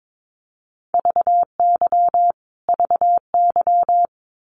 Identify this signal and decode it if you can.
Morse code, heard as a keyed tone, about 22 wpm, 700 Hz. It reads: VY VY
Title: Abbreviations Repeated 3x - 22wpm